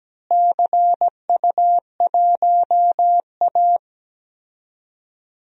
Morse code, heard as a keyed tone, about 17 wpm, 700 Hz. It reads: CU1A
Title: DXCC-100 Call Signs - 4 Characters - Repeated 3x - 17wpm